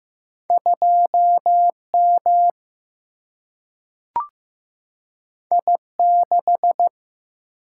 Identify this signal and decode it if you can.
Morse code, heard as a keyed tone, about 15 wpm, 700 Hz. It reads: 2M I6
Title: Single-Letter Single-Number 15wpm